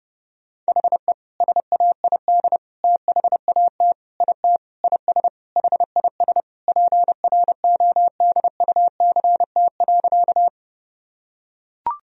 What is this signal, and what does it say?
Morse code, heard as a keyed tone, about 32 wpm, 700 Hz. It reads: HE SAID THAT IT IS HIS PRODUCT.
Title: Sentences from Top 1000 Words - Review Part 2 - 30wpm